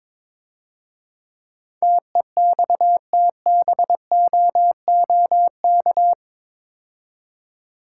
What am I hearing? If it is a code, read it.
Morse code, heard as a keyed tone, about 24 wpm, 700 Hz. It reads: TEXTBOOK
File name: Easy Words - Speed-Racing with Hints - 22wpm